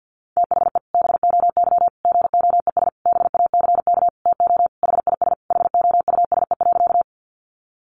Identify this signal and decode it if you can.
Morse code, heard as a keyed tone, about 50 wpm, 700 Hz. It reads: THE BOY GOES BACK TO HIS HOUSE.
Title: Easy Phrases 50wpm